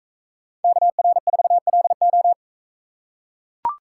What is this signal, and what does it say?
Morse code, heard as a keyed tone, about 40 wpm, 700 Hz. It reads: KR4LO